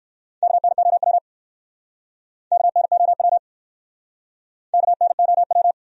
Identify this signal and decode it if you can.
Morse code, heard as a keyed tone, about 59 wpm, 700 Hz. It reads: KNOW KNOW KNOW